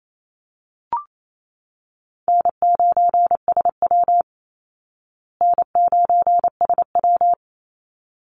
Morse code, heard as a keyed tone, about 28 wpm, 700 Hz. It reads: N9SW N9SW
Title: Easy Call Signs Repeated 2x - 28wpm